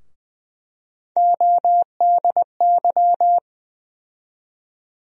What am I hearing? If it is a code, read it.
Morse code, heard as a keyed tone, about 20 wpm, 700 Hz. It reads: ODY
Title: USA Call Sign Postfixes - 3 Letters 20wpm